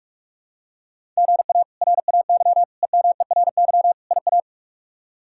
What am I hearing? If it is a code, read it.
Morse code, heard as a keyed tone, about 41 wpm, 700 Hz. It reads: GA RAY EMERY IA
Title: K1USN SST QSO Practice - 45wpm